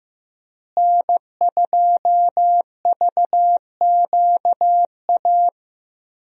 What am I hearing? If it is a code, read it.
Morse code, heard as a keyed tone, about 15 wpm, 700 Hz. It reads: N2VQA
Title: USA Call Signs - 5 Characters - Repeated 2x - 15wpm